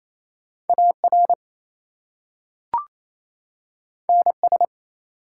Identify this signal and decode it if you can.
Morse code, heard as a keyed tone, about 29 wpm, 700 Hz. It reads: AR NS
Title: US States and Canadian Provinces 28wpm